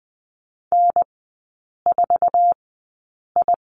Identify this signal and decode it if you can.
Morse code, heard as a keyed tone, about 20 wpm, spaced as for 8 wpm, 700 Hz. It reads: N4I
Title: 093 Lesson 16e - Letter c - Three Characters 20wpm